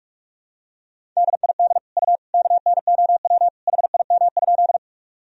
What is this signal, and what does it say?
Morse code, heard as a keyed tone, about 46 wpm, 700 Hz. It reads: DID U KNOW HIM?
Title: Sentences from Top 100 Words and Abbreviations 45wpm